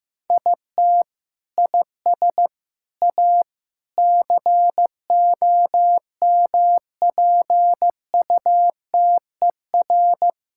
Morse code, heard as a keyed tone, about 15 wpm, 700 Hz. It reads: IT IS A COMPUTER
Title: Sentences from Top 1000 Words - Review Part 6 - 15wpm